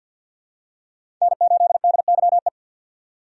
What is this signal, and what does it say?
Morse code, heard as a keyed tone, about 52 wpm, 700 Hz. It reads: N8DYE
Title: USA Call Signs - 5 Characters - 50wpm